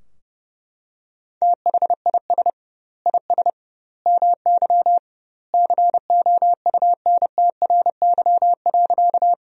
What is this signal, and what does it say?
Morse code, heard as a keyed tone, about 30 wpm, 700 Hz. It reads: THIS IS MY COUNTRY.
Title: 2x Word-Spacing - Sentences from Top 300 Words - 30wpm